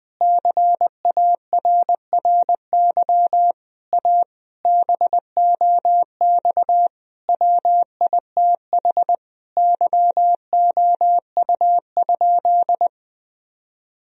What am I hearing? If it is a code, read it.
Morse code, heard as a keyed tone, about 20 wpm, 700 Hz. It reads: CARRY A BOX WITH YOU?
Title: Sentences from Top 1000 Words - Review Part 2 - 20wpm